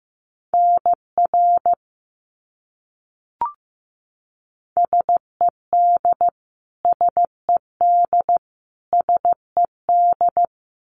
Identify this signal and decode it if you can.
Morse code, heard as a keyed tone, about 14 wpm, 700 Hz. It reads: NR SED SED SED
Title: Abbreviations Repeated 3x - 15wpm